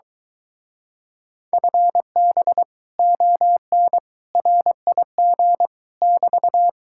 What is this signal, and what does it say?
Morse code, heard as a keyed tone, about 23 wpm, 700 Hz. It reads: FB ON RIG <BT>